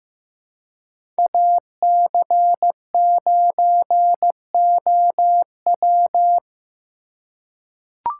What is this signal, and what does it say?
Morse code, heard as a keyed tone, about 15 wpm, 700 Hz. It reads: AC9OW
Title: USA Call Signs - 5 Characters - 4x Character Spacing - 15wpm